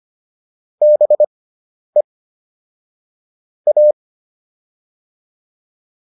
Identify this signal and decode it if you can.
Morse code, heard as a keyed tone, about 25 wpm, spaced as for 10 wpm, 600 Hz. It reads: BE A